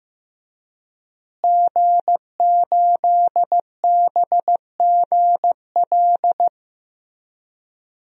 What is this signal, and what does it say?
Morse code, heard as a keyed tone, about 15 wpm, 700 Hz. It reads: G8BGL
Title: DXCC-100 Call Signs - 5 Characters - Repeated 3x - 15wpm